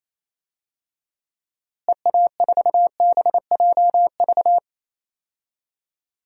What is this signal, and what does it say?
Morse code, heard as a keyed tone, about 28 wpm, 700 Hz. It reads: EA4BJV
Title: DXCC-100 Call Signs - 6 Characters - Repeated 3x - 28wpm